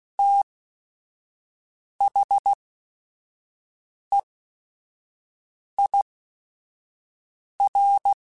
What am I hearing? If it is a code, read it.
Morse code, heard as a keyed tone, about 16 wpm, spaced as for 5 wpm, 800 Hz. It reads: THEIR
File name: T then H Runs / T then H Runs.mp3s / T then H Runs @5